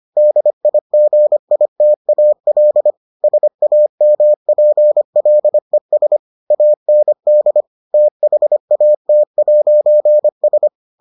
Morse code, heard as a keyed tone, about 25 wpm, 600 Hz. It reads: DIGITAL SAMPLES AND THAT'S